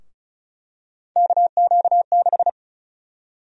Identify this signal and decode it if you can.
Morse code, heard as a keyed tone, about 36 wpm, 700 Hz. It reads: KQ6
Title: USA Call Sign Prefixes - 35wpm